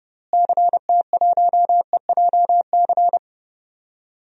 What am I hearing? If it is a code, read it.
Morse code, heard as a keyed tone, about 30 wpm, 700 Hz. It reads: CT1EJC
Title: DXCC-100 Call Signs - 6 Characters - Repeated 2x - 30wpm